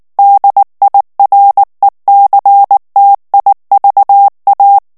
Morse code, heard as a keyed tone, about 19 wpm, 800 Hz. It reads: DIRECTIVA
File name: morse mp3 / ESPAGNOL TEXTE EN CLAIR 2 VITESSE 18 MOTS MINUTES